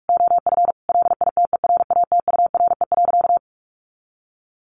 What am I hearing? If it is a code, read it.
Morse code, heard as a keyed tone, about 44 wpm, 700 Hz. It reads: OF LITERATURE.
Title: Sentences from Top 900 Words 45wpm